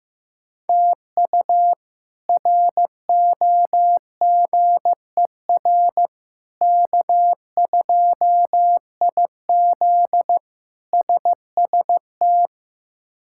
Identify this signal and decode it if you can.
Morse code, heard as a keyed tone, about 16 wpm, 700 Hz. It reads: TU ROGER K2IZ SST
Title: K1USN SST QSO Practice - 15wpm